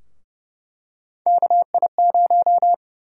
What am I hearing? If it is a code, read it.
Morse code, heard as a keyed tone, about 30 wpm, 700 Hz. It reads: KI0
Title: USA Call Sign Prefixes - Repeated 2x - 30wpm